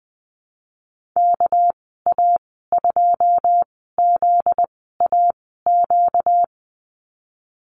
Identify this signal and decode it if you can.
Morse code, heard as a keyed tone, about 20 wpm, spaced as for 15 wpm, 700 Hz. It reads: KA2ZAQ
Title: 245 Lesson 39g - Letter z - Call Signs 20wpm